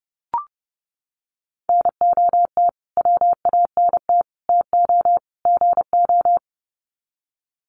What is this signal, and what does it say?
Morse code, heard as a keyed tone, about 30 wpm, 700 Hz. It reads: NOT WANT TO GO
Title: Sets of 4 Words 30wpm